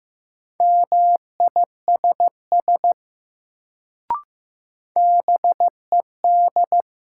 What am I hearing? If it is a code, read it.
Morse code, heard as a keyed tone, about 15 wpm, 700 Hz. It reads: MISS BED
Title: Sentences from Top 500 Words - New Word Drill - 15wpm